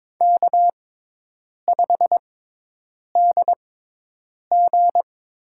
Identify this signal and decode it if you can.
Morse code, heard as a keyed tone, about 22 wpm, spaced as for 8 wpm, 700 Hz. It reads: K5DG
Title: USA Call Signs - 6x Character Spacing - 22wpm